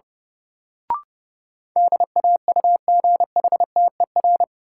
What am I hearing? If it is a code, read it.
Morse code, heard as a keyed tone, about 32 wpm, 700 Hz. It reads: DAUGHTER
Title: Sentences from Top 1500 Words - New Word Drill - 30wpm